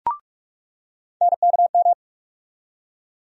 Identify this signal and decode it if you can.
Morse code, heard as a keyed tone, about 46 wpm, 700 Hz. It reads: NKM
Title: USA Call Sign Postfixes - 3 Letters 45wpm